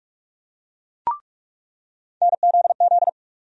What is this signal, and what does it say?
Morse code, heard as a keyed tone, about 46 wpm, 700 Hz. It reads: NZZ
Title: USA Call Sign Postfixes - 3 Letters 45wpm